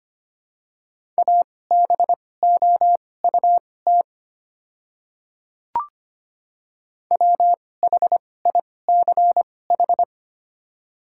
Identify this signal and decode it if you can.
Morse code, heard as a keyed tone, about 25 wpm, spaced as for 18 wpm, 700 Hz. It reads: ABOUT WHICH
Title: 2x Character Spacing - Top 200 Words - 25wpm